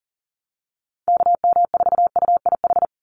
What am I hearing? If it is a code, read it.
Morse code, heard as a keyed tone, about 38 wpm, 700 Hz. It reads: KM4UIH
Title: USA Call Signs 40wpm